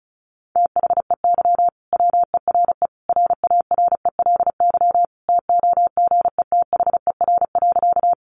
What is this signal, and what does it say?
Morse code, heard as a keyed tone, about 34 wpm, 700 Hz. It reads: THEY WERE RARELY TOGETHER.
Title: Sentences from Top 1400 Words 35wpm